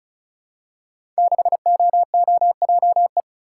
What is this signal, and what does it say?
Morse code, heard as a keyed tone, about 34 wpm, 700 Hz. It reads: 6OOJE
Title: Code Groups - 5 Letter:Numbers - 35wpm